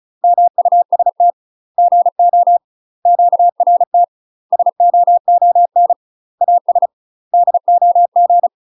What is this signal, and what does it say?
Morse code, heard as a keyed tone, about 36 wpm, 700 Hz. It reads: MUST GO QRT SOON AS DOG
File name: UK MASTER QSOs TRAINING / 35WPM / UK MASTER QSOs 0007